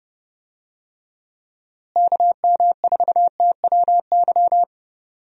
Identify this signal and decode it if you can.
Morse code, heard as a keyed tone, about 30 wpm, 700 Hz. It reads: KM4TWY
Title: USA Call Signs - 6 Characters - Repeated 2x - 30wpm